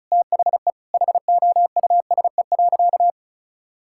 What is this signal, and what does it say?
Morse code, heard as a keyed tone, about 36 wpm, 700 Hz. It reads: THE HOUSE.
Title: Sentences from Top 1000 Words - Review Part 6 - 35wpm